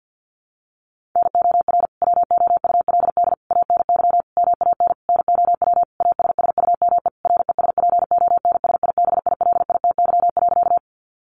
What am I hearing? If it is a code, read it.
Morse code, heard as a keyed tone, about 50 wpm, 700 Hz. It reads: NOR WOULD ANY MAN NOW ASSUME RESPONSIBILITY.
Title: Sentences from Top 1500 Words 50wpm